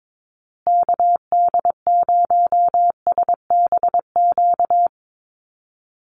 Morse code, heard as a keyed tone, about 22 wpm, 700 Hz. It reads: KD0SBQ
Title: USA Call Signs Repeated 3x 22wpm